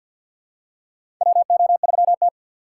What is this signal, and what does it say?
Morse code, heard as a keyed tone, about 50 wpm, 700 Hz. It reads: WO3T
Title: USA Call Signs - 4 Characters - 2x Character Spacing - 50wpm